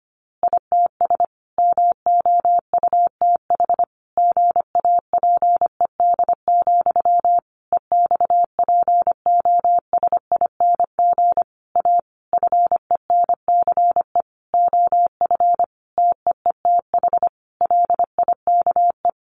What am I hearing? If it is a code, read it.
Morse code, heard as a keyed tone, about 25 wpm, 700 Hz. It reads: ITS MOUTH GAPED, EXPOSING A FENCE OF TEETH LIKE
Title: A Sound Of Thunder 25wpm